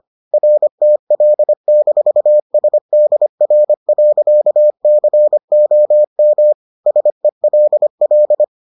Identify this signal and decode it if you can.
Morse code, heard as a keyed tone, about 25 wpm, 600 Hz. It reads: RTL-SDR.COM SELL